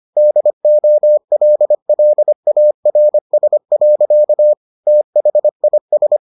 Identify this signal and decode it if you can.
Morse code, heard as a keyed tone, about 25 wpm, 600 Hz. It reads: DOLLARS. THIS